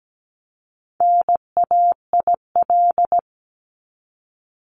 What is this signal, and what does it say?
Morse code, heard as a keyed tone, about 17 wpm, 700 Hz. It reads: NAIL
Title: Top 1501-1750 Words 17wpm